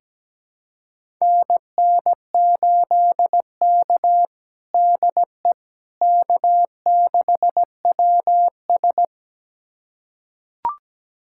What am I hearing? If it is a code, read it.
Morse code, heard as a keyed tone, about 17 wpm, 700 Hz. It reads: NN8K DE K6WS
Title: USA Callsign DE Callsign - 4 Characters - 17wpm